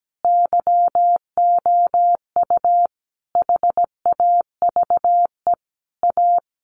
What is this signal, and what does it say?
Morse code, heard as a keyed tone, about 17 wpm, 700 Hz. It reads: YOU HAVE A
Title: Sentences from Top 1400 Words 17wpm